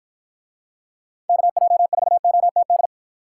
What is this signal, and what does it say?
Morse code, heard as a keyed tone, about 48 wpm, 700 Hz. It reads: KJ4OTD